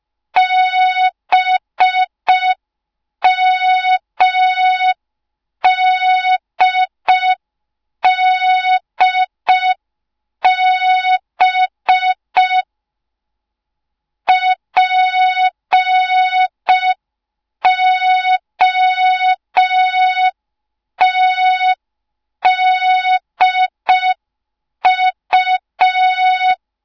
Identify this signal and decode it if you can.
Morse code, heard as a keyed tone, about 5 wpm, 3700 Hz, with noise. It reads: BMDDB POTDU
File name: Morze-HA2MN / hang / 06BDU HA2MN